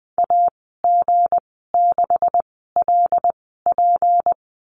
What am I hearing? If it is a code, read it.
Morse code, heard as a keyed tone, about 20 wpm, spaced as for 15 wpm, 700 Hz. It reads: AG6LP
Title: 179 Lesson 29g - Slash - Call Signs 20wpm